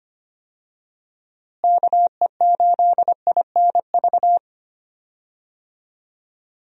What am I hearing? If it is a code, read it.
Morse code, heard as a keyed tone, about 25 wpm, 700 Hz. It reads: KE8INV
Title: USA Call Signs - 6 Characters - Repeated 3x - 25wpm